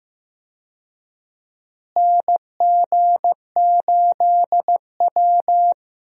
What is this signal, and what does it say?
Morse code, heard as a keyed tone, about 15 wpm, 700 Hz. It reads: NG8W
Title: USA Call Signs - 4 Characters - Repeated 2x - 15wpm